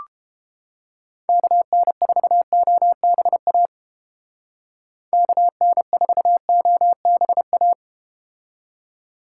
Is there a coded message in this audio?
Yes: Morse code from 1.3 to 7.7 s, heard as a keyed tone, about 30 wpm, 700 Hz. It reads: KN4OBA KN4OBA